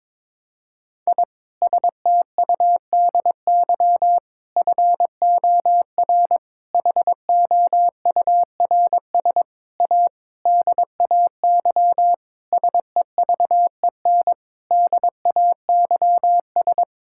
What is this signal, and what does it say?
Morse code, heard as a keyed tone, about 22 wpm, 700 Hz. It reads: I STUDY FOR HOURS A DAY SEVEN DAYS
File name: Sentences with 7-Word Phrases - 22wpm